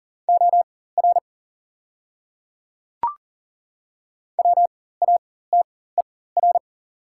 Morse code, heard as a keyed tone, about 40 wpm, spaced as for 19 wpm, 700 Hz. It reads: OR WATER
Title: 4x Character Spacing - Top 100 Words - 40wpm